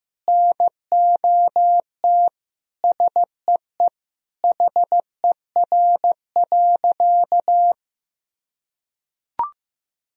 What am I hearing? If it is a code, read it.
Morse code, heard as a keyed tone, about 15 wpm, 700 Hz. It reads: NOT SEE HER.
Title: Sentences from Top 1000 Words - Review Part 4 - 15wpm